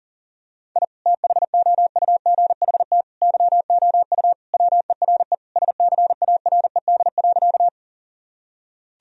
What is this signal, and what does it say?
Morse code, heard as a keyed tone, about 40 wpm, 700 Hz. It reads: I THOUGHT YOU WERE SCARED.
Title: Sentences from Top 1000 Words - Review Part 4 - 40wpm